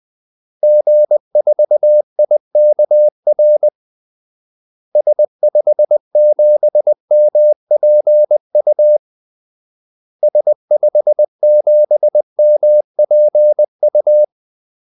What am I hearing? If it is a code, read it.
Morse code, heard as a keyed tone, about 20 wpm, 600 Hz. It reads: G4IKR S57MPU S57MPU
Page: Kaldesignaler 20 wpm